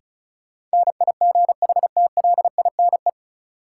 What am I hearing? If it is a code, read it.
Morse code, heard as a keyed tone, about 36 wpm, 700 Hz. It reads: NIGHTLINE
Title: Sets of 5 Words - New Word Drill - 35wpm